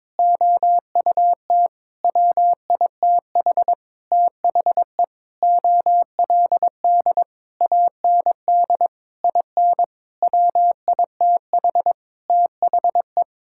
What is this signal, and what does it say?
Morse code, heard as a keyed tone, about 22 wpm, 700 Hz. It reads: OUT WITH THE OLD AND IN WITH THE
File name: Sentences with 9-Word Phrases - 22wpm